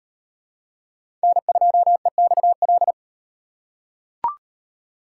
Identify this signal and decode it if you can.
Morse code, heard as a keyed tone, about 38 wpm, 700 Hz. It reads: N2EXL